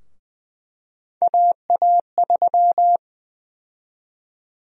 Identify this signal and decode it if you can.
Morse code, heard as a keyed tone, about 21 wpm, 700 Hz. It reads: AA3
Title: USA Call Sign Prefixes - Repeated 2x - 20wpm